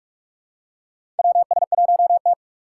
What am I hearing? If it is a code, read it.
Morse code, heard as a keyed tone, about 45 wpm, 700 Hz. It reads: WS1T
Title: USA Call Signs - 4 Characters - 4x Character Spacing - 45wpm